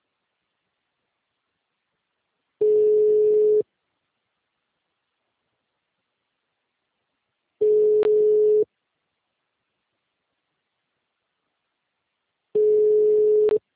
Ringback tone (Europe, 425 Hz).
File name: out- -2004